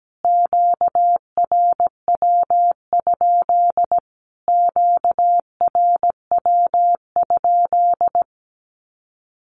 Morse code, heard as a keyed tone, about 17 wpm, 700 Hz. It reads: QRW? QRW?
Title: Q-Codes Repeated 3x 17wpm